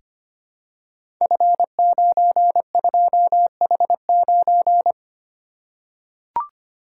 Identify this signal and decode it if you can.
Morse code, heard as a keyed tone, about 25 wpm, 700 Hz. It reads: F92H9
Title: Code Groups - 5 Letter:Numbers - 25wpm